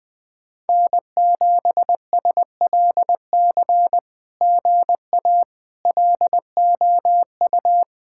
Morse code, heard as a keyed tone, about 20 wpm, 700 Hz. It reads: N7SLC GA LOU
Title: SOTA QSO Practice 20wpm